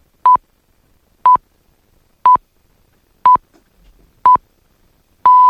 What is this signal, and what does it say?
Radio time signal (time pips). It is heard from 0.3 to 4.7 s.